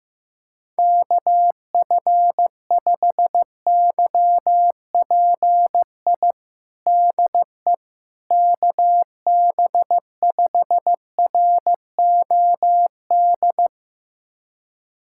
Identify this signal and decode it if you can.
Morse code, heard as a keyed tone, about 15 wpm, 700 Hz. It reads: KF5YPI DE KB5ROD
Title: USA Callsign DE Callsign 15wpm